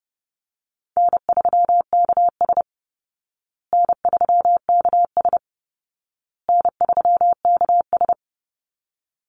Morse code, heard as a keyed tone, about 28 wpm, 700 Hz. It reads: N3KS N3KS N3KS
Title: Easy Call Signs Repeated 3x - 30wpm